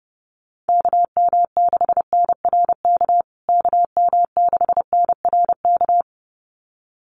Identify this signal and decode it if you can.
Morse code, heard as a keyed tone, about 30 wpm, 700 Hz. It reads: KM6NRK KM6NRK
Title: USA Call Signs Repeated 2x 30wpm